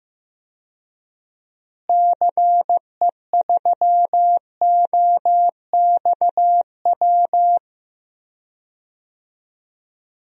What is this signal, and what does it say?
Morse code, heard as a keyed tone, about 15 wpm, 700 Hz. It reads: CE3OXW
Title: DXCC-100 Call Signs - 6 Characters - Repeated 3x - 15wpm